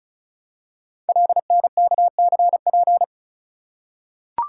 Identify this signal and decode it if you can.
Morse code, heard as a keyed tone, about 36 wpm, 690 Hz. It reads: LNKCP
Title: Code Groups - 5 Letter:Numbers - 35wpm